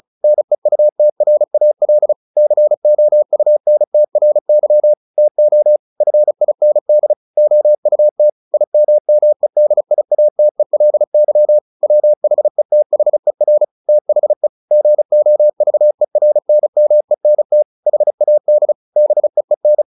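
Morse code, heard as a keyed tone, about 35 wpm, 600 Hz. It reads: NEUTRAL COUNTRY TO FIND OUT IMMEDIATELY WHETHER THE GOVERNMENT HAD BEEN
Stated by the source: Morse practice at 35 wpm from CW med Gnister, sending Ken Follett's novel Never in English.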